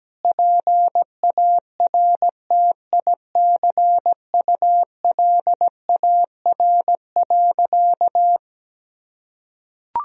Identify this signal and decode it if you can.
Morse code, heard as a keyed tone, about 17 wpm, 700 Hz. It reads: PARTICULAR.
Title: Sentences from Top 1000 Words - Review Part 4 - 17wpm